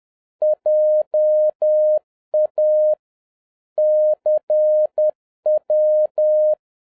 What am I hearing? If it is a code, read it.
Morse code, heard as a keyed tone, about 10 wpm, 610 Hz. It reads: JA CW
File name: 2011-07-mp3 / n21 GREETINGS FROM SINGAPORE 10wpm